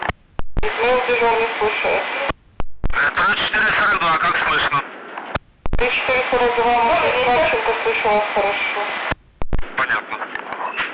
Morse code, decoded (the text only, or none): none